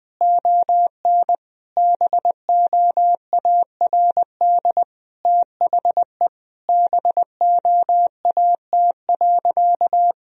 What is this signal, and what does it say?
Morse code, heard as a keyed tone, about 20 wpm, 700 Hz. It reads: ON BOARD THE BOAT.
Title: Sentences from Top 1000 Words - Review Part 4 - 20wpm